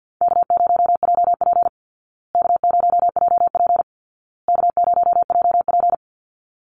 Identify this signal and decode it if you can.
Morse code, heard as a keyed tone, about 50 wpm, 700 Hz. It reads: K0JP K0JP K0JP
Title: Call Signs Repeated 2x - 50wpm